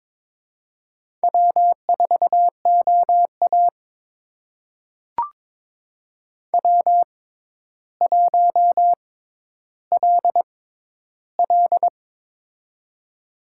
Morse code, heard as a keyed tone, about 22 wpm, spaced as for 11 wpm, 700 Hz. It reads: W4OA W1LL
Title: USA Call Signs - 6x Character Spacing - 22wpm